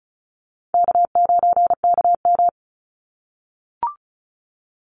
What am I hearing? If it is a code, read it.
Morse code, heard as a keyed tone, about 34 wpm, 700 Hz. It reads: K9KM
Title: Call Signs - 35wpm